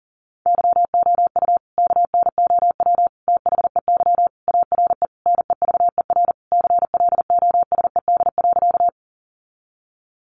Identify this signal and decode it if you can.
Morse code, heard as a keyed tone, about 40 wpm, 700 Hz. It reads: YOU KNOW THEY ARE NEVER CLOSED.
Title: Sentences from Top 1400 Words 40wpm